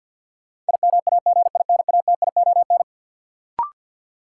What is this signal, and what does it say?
Morse code, heard as a keyed tone, about 52 wpm, 700 Hz. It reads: IMAGINATION
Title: Sentences from Top 1300 Words - New Word Drill Alt - 50wpm